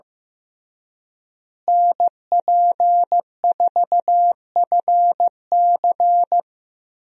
Morse code, heard as a keyed tone, about 15 wpm, 700 Hz. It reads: NP4FC
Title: USA Call Signs - 5 Characters - Repeated 3x - 15wpm